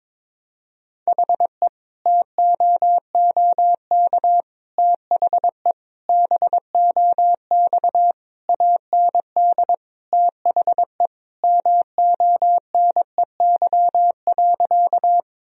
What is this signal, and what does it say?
Morse code, heard as a keyed tone, about 22 wpm, 700 Hz. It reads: HE TOOK THE BOX AND THE MONEY.
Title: Sentences from Top 1000 Words - Review Part 2 - 22wpm